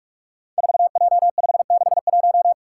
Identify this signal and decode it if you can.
Morse code, heard as a keyed tone, about 44 wpm, 700 Hz. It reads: 4J561
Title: Five Challenging Letter-Number Combinations 45wpm